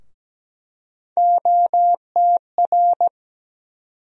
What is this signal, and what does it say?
Morse code, heard as a keyed tone, about 17 wpm, 700 Hz. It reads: OTR
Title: USA Call Sign Postfixes - 3 Letters 17wpm